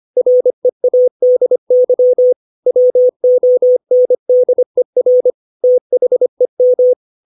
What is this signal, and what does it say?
Morse code, heard as a keyed tone, about 25 wpm, 500 Hz. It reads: READY WONDER THEM
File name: 632 words / 632 Words 25WPM